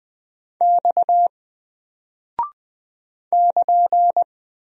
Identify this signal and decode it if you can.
Morse code, heard as a keyed tone, about 20 wpm, 700 Hz. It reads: X <KN>
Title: Single Character (Letter:Number:Punctuation:Prosign) Alt 20wpm